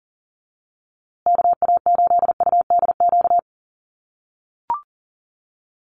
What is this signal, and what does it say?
Morse code, heard as a keyed tone, about 40 wpm, 700 Hz. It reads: KA8UDQ
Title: USA Call Signs Repeated 3x 40wpm